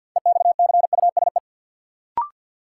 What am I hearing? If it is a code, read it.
Morse code, heard as a keyed tone, about 54 wpm, 700 Hz. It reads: EXCUSE
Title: Sentences from Top 700 Words - New Word Drill - 50wpm